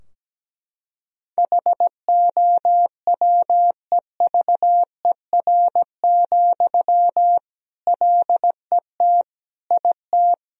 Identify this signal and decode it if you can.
Morse code, heard as a keyed tone, about 17 wpm, 700 Hz. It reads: HOWEVER, LET IT
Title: Sentences from Top 1000 Words - Review Part 2 - 17wpm